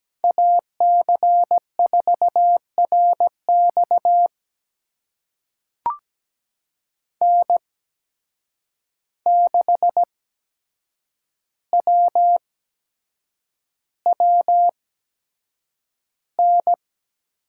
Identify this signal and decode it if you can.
Morse code, heard as a keyed tone, about 17 wpm, spaced as for 7 wpm, 700 Hz. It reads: AC4RX N6WWN
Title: USA Call Signs - 5 Characters - 8x Character Spacing - 17wpm